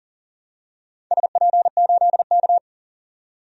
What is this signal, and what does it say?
Morse code, heard as a keyed tone, about 41 wpm, 700 Hz. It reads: SP8K
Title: DXCC-100 Call Signs - 4 Characters - 40wpm